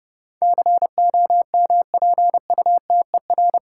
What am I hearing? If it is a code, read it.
Morse code, heard as a keyed tone, about 29 wpm, 700 Hz. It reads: COMPUTER
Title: Sentences from Top 1000 Words - Review Part 3 - New Word Drill - 30wpm